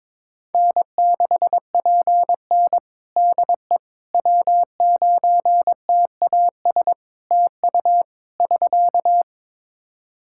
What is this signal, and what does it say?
Morse code, heard as a keyed tone, about 23 wpm, 700 Hz. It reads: N6PN DE W9TAS TU <SK>
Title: Practice QSO - Exchange 4 - Ending - 22wpm